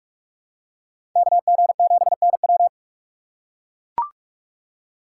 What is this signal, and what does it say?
Morse code, heard as a keyed tone, about 43 wpm, 700 Hz. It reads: KG7NW
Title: USA Call Signs - 5 Characters - 3x Character Spacing - 45wpm